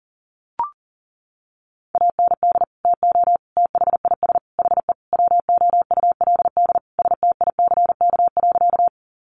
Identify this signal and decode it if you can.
Morse code, heard as a keyed tone, about 40 wpm, 700 Hz. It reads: AND TO THIS HE WOULD STICK.
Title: Sentences from Top 1200 Words 40wpm